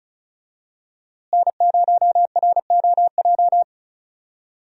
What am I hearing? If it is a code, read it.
Morse code, heard as a keyed tone, about 34 wpm, 700 Hz. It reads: N0ROJ
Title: USA Call Signs - 5 Characters - Repeated 3x - 35wpm